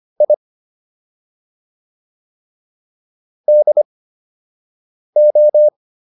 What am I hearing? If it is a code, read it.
Morse code, heard as a keyed tone, about 25 wpm, spaced as for 6 wpm, 600 Hz. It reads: I DO